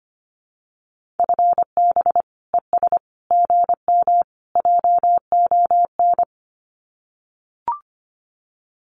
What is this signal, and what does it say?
Morse code, heard as a keyed tone, about 25 wpm, 700 Hz. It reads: FB ES GM JON
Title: Common QSO Phrases Encore 25wpm